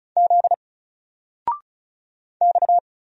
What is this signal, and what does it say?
Morse code, heard as a keyed tone, about 35 wpm, 700 Hz. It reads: Z X
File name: Single-Letter-Number-35wpm